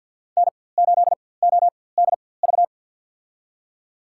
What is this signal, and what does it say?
Morse code, heard as a keyed tone, about 46 wpm, spaced as for 23 wpm, 700 Hz. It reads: N8ODV
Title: USA Call Signs - 5 Characters - 4x Character Spacing - 50wpm